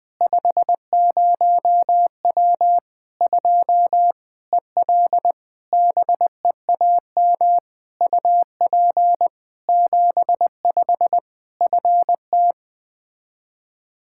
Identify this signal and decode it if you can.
Morse code, heard as a keyed tone, about 20 wpm, 700 Hz. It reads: 50W 2 EL BEAM UP 75 FT
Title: Practice QSO - Exchange 2 - Info - 20wpm